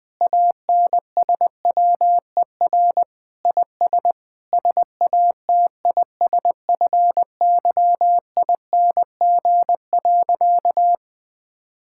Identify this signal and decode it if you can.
Morse code, heard as a keyed tone, about 21 wpm, 700 Hz. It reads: ANSWER IS SATISFYING.
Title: Sentences with 3-Word Phrases - 20wpm